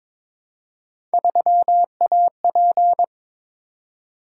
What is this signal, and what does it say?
Morse code, heard as a keyed tone, about 22 wpm, 700 Hz. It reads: 3AP
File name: practice / 22wpm / Three Letter-Number Combinations - 22wpm